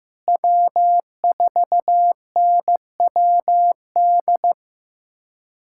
Morse code, heard as a keyed tone, about 15 wpm, 700 Hz. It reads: W4NWD
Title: USA Callsign DE Callsign - 5 Characters - 15wpm